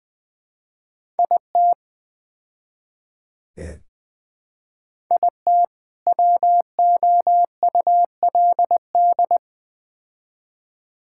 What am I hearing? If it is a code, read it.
Morse code, heard as a keyed tone, about 20 wpm, 700 Hz. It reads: IT IT WOULD
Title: Build A Phrase - 4-Word Phrases - 20wpm